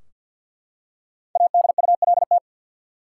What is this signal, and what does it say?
Morse code, heard as a keyed tone, about 52 wpm, 700 Hz. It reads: ADULT
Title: Five Letter Words - 50wpm